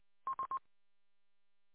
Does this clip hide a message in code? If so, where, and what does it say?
No.